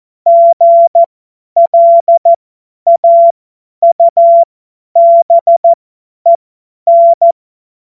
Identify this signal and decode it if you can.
Morse code, heard as a keyed tone, about 14 wpm, spaced as for 10 wpm, 680 Hz. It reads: GLAUBEN
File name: Morsekurs / Witze / Witz 11. Juli 2023 50 F70